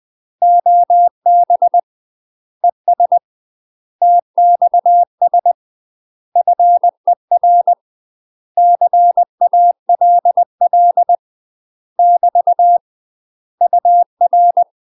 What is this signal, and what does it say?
Morse code, heard as a keyed tone, about 20 wpm, 700 Hz. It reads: OB ES TXS FER CALL <BT> UR
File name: UK MASTER QSOs TRAINING / 20WPM / UK MASTER QSOs 0068